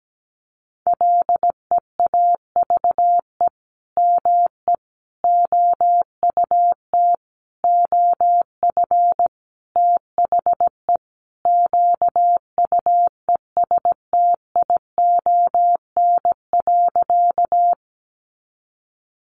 Morse code, heard as a keyed tone, about 17 wpm, 700 Hz. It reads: LEAVE ME OUT OF THE QUESTION.
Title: Sentences from Top 400 Words 17wpm